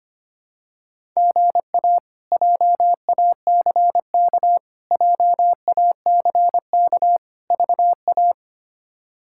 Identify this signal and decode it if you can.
Morse code, heard as a keyed tone, about 25 wpm, 700 Hz. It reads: GA JACK JACK VA
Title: K1USN SST QSO Practice - 25wpm